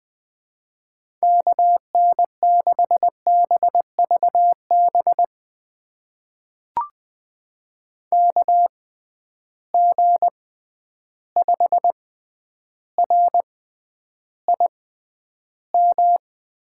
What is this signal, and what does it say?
Morse code, heard as a keyed tone, about 20 wpm, spaced as for 10 wpm, 700 Hz. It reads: KN6BVB KG5RIM
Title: USA Call Signs - 6 Characters - 6x Character Spacing - 20wpm